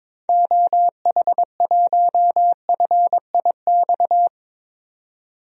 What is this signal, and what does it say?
Morse code, heard as a keyed tone, about 22 wpm, 700 Hz. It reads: OH1FIX
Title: DXCC-100 Call Signs - 6 Characters - 22wpm